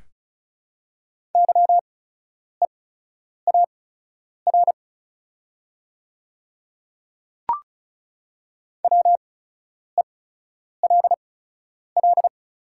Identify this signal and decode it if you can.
Morse code, heard as a keyed tone, about 34 wpm, spaced as for 10 wpm, 700 Hz. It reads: YEAR WELL
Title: 8x Character Spacing - Top 300 Words - 35wpm